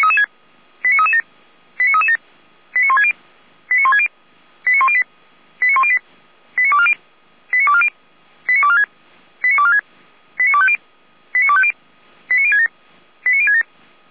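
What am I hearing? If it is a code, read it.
Five-tone selective call (ZVEI), heard as quick runs of five tones, tones of about 70 ms. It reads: 89337 89337 89160 89160 89108 89108 89240 89240 89246 89246 89250 89250 89067 89067
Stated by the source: Five-tone ZVEI selective call 89075 received by SDR.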